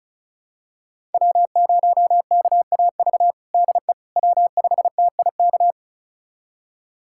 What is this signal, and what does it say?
Morse code, heard as a keyed tone, about 36 wpm, 700 Hz. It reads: W0KAV DE W5TIK
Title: USA Callsign DE Callsign - 5 Characters - 35wpm